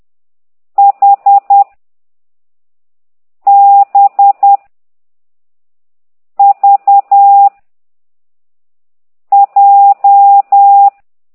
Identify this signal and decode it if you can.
Morse code, heard as a keyed tone, about 10 wpm, 800 Hz. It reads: H B V J